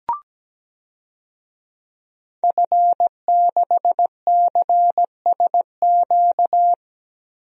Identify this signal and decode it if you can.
Morse code, heard as a keyed tone, about 17 wpm, 700 Hz. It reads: F6CSQ
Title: DXCC-100 Call Signs - 5 Characters - 17wpm